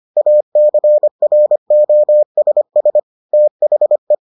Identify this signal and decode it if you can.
Morse code, heard as a keyed tone, about 25 wpm, 600 Hz. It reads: ACROSS THE